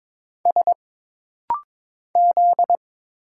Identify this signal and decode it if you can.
Morse code, heard as a keyed tone, about 22 wpm, 700 Hz. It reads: S Z
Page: This rapid-fire timing is geared to push your ICR (Instant Character Recognition) skills.